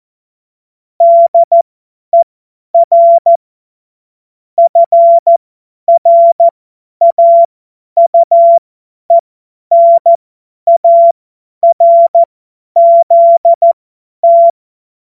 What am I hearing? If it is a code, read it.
Morse code, heard as a keyed tone, about 14 wpm, spaced as for 10 wpm, 680 Hz. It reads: DER FRAUENARZT
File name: Morsekurs / Witze / Witz 12 März 2024 50 F70